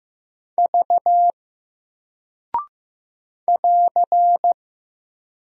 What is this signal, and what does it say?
Morse code, heard as a keyed tone, about 15 wpm, 700 Hz. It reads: V <AR>
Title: Single Character (Letter:Number:Punctuation:Prosign) 15wpm